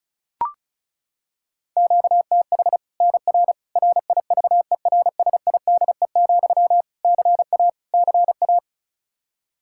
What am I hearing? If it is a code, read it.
Morse code, heard as a keyed tone, about 36 wpm, 700 Hz. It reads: QTH NR RIVERSIDE, CA CA
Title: Practice QSO - Exchange 1 - Beginning - 35wpm